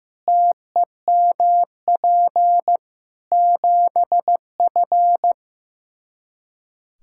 Morse code, heard as a keyed tone, about 15 wpm, 700 Hz. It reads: TEMP 7F
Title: Practice QSO - Exchange 2 - Info - 15wpm